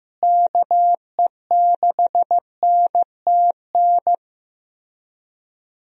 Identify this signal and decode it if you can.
Morse code, heard as a keyed tone, about 15 wpm, 700 Hz. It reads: KE6NTN
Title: USA Call Signs - 6 Characters - Repeated 3x - 15wpm